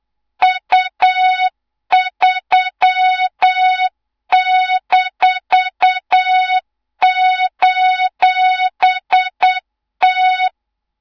Morse code, heard as a keyed tone, about 8 wpm, 3700 Hz, with noise. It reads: U3-:T
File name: Morze-HA2MN / hang / 24 GYAKORLAT